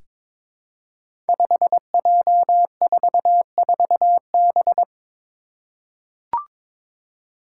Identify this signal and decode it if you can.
Morse code, heard as a keyed tone, about 22 wpm, 700 Hz. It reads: 5J44B
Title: Five Challenging Letter-Number Combinations 22wpm